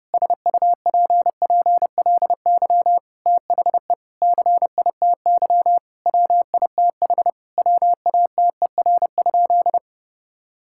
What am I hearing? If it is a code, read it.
Morse code, heard as a keyed tone, about 30 wpm, 700 Hz. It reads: SUPPLY THE CITY WITH WATER?
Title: Sentences from Top 1000 Words - Review Part 5 - 30wpm